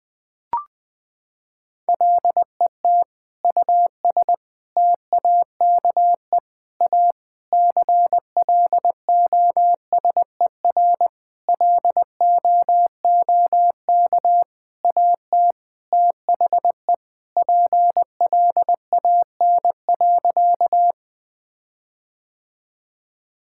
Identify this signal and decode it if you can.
Morse code, heard as a keyed tone, about 20 wpm, 700 Hz. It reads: LET US TAKE A CLOSER LOOK AT THE PLAN.
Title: Sentences with 6-Word Phrases - 20wpm